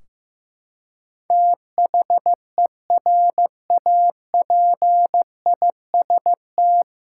Morse code, heard as a keyed tone, about 16 wpm, 700 Hz. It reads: THERAPIST
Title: K1USN SST QSO Practice 15wpm